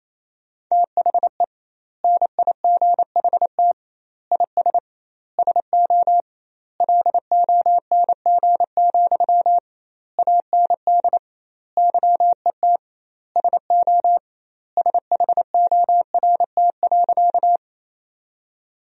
Morse code, heard as a keyed tone, about 28 wpm, 700 Hz. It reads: THE NIGHT IS SO LONG, AND YET SO SHORT.
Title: 2x Word-Spacing - Sentences from Top 500 Words - 28wpm